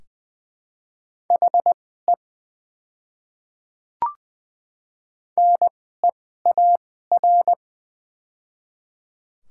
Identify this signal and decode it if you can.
Morse code, heard as a keyed tone, about 20 wpm, spaced as for 15 wpm, 700 Hz. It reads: HE NEAR
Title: 2x Character Spacing - Top 300 Words - 20wpm